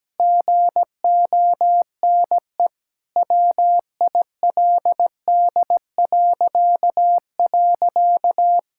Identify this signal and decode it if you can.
Morse code, heard as a keyed tone, about 17 wpm, 700 Hz. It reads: GONE WILD..
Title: Pangram Challenge v2 - 17wpm